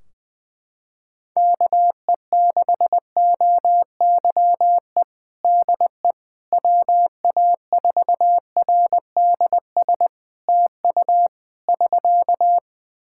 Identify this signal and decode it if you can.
Morse code, heard as a keyed tone, about 20 wpm, 700 Hz. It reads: KE6OYE DE WA4RDS TU <SK>
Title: Practice QSO - Exchange 4 - Ending - 20wpm